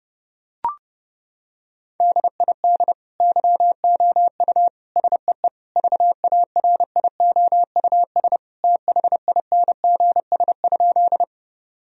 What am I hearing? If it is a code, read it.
Morse code, heard as a keyed tone, about 32 wpm, 700 Hz. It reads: DID YOU SEE VARIOUS THINGS?
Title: Sentences from Top 1000 Words - Review Part 4 - 30wpm